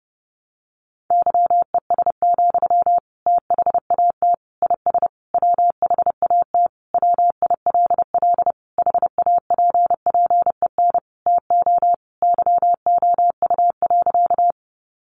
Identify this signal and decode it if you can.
Morse code, heard as a keyed tone, about 30 wpm, 700 Hz. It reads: YES, THAT IS WHAT WILL HAPPEN TO YOU.
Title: Sentences from Top 500 Words 30wpm